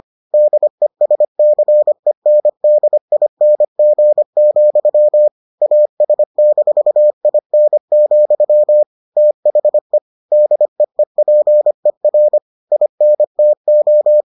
Morse code, heard as a keyed tone, about 25 wpm, 600 Hz. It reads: DESCENDING, AS-IN, THE DEEPER INTO